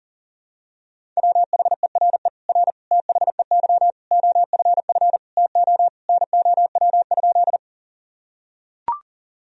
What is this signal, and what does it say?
Morse code, heard as a keyed tone, about 37 wpm, 700 Hz. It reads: WHERE R THEY OFF TO NOW?
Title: Sentences from Top 200 Words and Abbreviations 40wpm